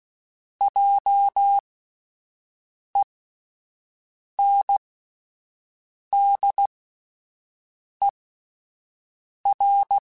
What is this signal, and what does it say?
Morse code, heard as a keyed tone, about 16 wpm, spaced as for 5 wpm, 800 Hz. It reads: JENDER
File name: Spaced Alphas / Spaced Alphas mp3s / Js sp @10